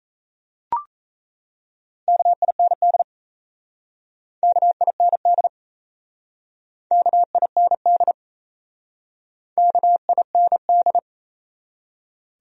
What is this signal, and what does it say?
Morse code, heard as a keyed tone, about 35 wpm, 700 Hz. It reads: KIND KIND KIND KIND